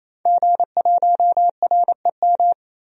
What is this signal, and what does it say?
Morse code, heard as a keyed tone, about 28 wpm, 700 Hz. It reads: G1REM
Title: DXCC-100 Call Signs - 5 Characters - 28wpm